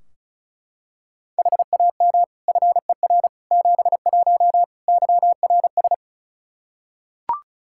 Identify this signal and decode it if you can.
Morse code, heard as a keyed tone, about 34 wpm, 700 Hz. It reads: HAM FER 71 YRS
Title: K1USN SST QSO Practice 35wpm